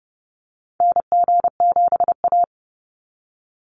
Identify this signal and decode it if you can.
Morse code, heard as a keyed tone, about 30 wpm, 700 Hz. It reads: NG7A
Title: NA SOTA Activator Callsigns 30wpm